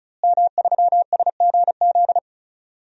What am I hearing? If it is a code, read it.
Morse code, heard as a keyed tone, about 34 wpm, 700 Hz. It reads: M3SGZ
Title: DXCC-100 Call Signs - 5 Characters - Repeated 2x - 35wpm